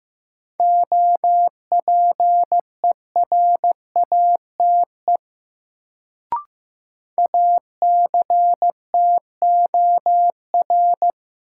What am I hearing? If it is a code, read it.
Morse code, heard as a keyed tone, about 15 wpm, 700 Hz. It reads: OPERATE ACTOR
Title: Sentences from Top 1500 Words - New Word Drill - 15wpm